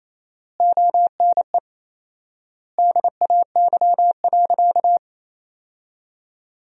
Morse code, heard as a keyed tone, about 28 wpm, 700 Hz. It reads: ONE DAY.
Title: 4x Word-Spacing - Sentences from Top 500 Words - 28wpm